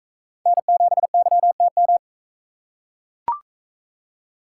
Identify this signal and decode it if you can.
Morse code, heard as a keyed tone, about 44 wpm, 700 Hz. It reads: N7YTM